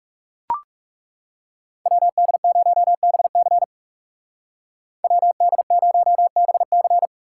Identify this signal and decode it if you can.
Morse code, heard as a keyed tone, about 39 wpm, 700 Hz. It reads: WD0BC WD0BC